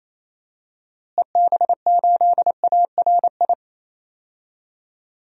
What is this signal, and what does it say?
Morse code, heard as a keyed tone, about 28 wpm, 700 Hz. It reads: EB8ARI
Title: DXCC-100 Call Signs - 6 Characters - Repeated 3x - 28wpm